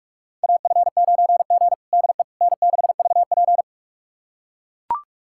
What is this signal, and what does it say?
Morse code, heard as a keyed tone, about 42 wpm, 700 Hz. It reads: AF9G DE N6VP
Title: USA Callsign DE Callsign - 4 Characters - 45wpm